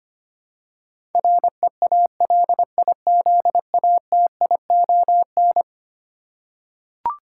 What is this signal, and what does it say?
Morse code, heard as a keyed tone, about 25 wpm, 700 Hz. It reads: REALIZATION
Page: morse-code-ninja-practice-25wpm